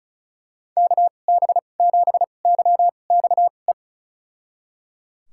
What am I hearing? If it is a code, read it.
Morse code, heard as a keyed tone, about 34 wpm, spaced as for 25 wpm, 700 Hz. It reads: KB7YXE
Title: USA Call Signs - 6 Characters - 2x Character Spacing - 35wpm